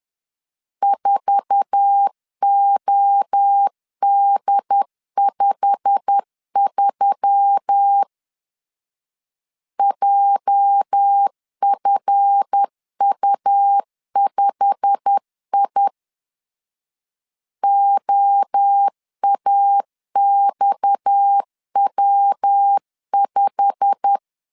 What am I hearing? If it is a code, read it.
Morse code, heard as a keyed tone, about 10 wpm, 800 Hz. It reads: 4OD53 JFU5I OAXW5
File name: LEZIONI / 8 YJ45 / 8 YJ45 10wpm 2549